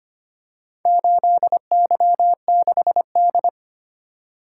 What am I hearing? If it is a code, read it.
Morse code, heard as a keyed tone, about 25 wpm, 700 Hz. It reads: 8Y6D
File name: practice / 25wpm / Four Letter-Number Combinations - 25wpm